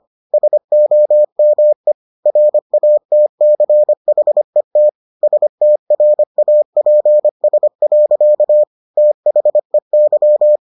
Morse code, heard as a keyed tone, about 25 wpm, 600 Hz. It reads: SOME RATCHET STRAPS. THEY